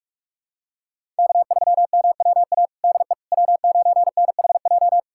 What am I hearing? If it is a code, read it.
Morse code, heard as a keyed tone, about 44 wpm, 700 Hz. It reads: K3MWA DE W9NHJ
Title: USA Callsign DE Callsign - 5 Characters - 45wpm